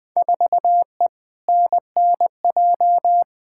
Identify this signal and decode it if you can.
Morse code, heard as a keyed tone, about 20 wpm, 700 Hz. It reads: 4E NNJ
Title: Field Day Exchanges - 20wpm